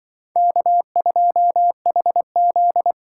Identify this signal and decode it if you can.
Morse code, heard as a keyed tone, about 23 wpm, 700 Hz. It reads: K2HZ
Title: Speed Racing - USA 1x2 Call Signs 20wpm